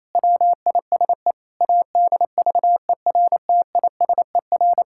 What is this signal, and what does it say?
Morse code, heard as a keyed tone, about 28 wpm, 700 Hz. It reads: WISE ADVERTISER
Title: Pangram Challenge v2 - 28wpm